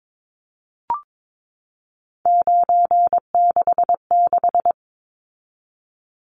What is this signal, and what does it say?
Morse code, heard as a keyed tone, about 22 wpm, 700 Hz. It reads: 966
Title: Numbers - 3-Digit 22wpm